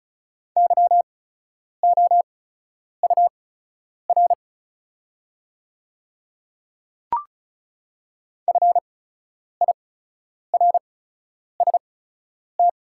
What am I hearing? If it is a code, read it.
Morse code, heard as a keyed tone, about 36 wpm, spaced as for 10 wpm, 700 Hz. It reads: YOUR FIRST
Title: 8x Character Spacing - Top 100 Words - 35wpm